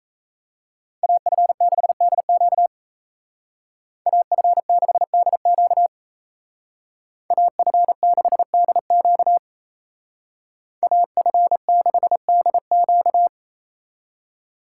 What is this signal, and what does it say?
Morse code, heard as a keyed tone, about 33 wpm, 700 Hz. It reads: AF6DQ AF6DQ AF6DQ AF6DQ
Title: Speed Racing - USA 2x2 Call Signs 28wpm